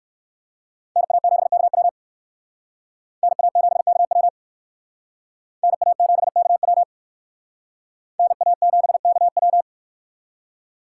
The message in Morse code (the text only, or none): NA7KW NA7KW NA7KW NA7KW